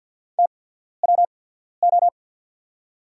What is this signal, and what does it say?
Morse code, heard as a keyed tone, about 50 wpm, spaced as for 14 wpm, 690 Hz. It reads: TWO
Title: 8x Character Spacing - Top 100 Words - 50wpm